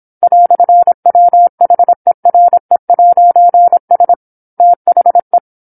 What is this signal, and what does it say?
Morse code, heard as a keyed tone, about 27 wpm, 700 Hz. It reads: "WHERE'S THE
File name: 08-Chapter-08 26 WPM